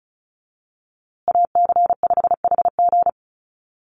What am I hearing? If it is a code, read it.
Morse code, heard as a keyed tone, about 34 wpm, 700 Hz. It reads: AC5HG
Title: USA Call Signs Repeated 2x 35wpm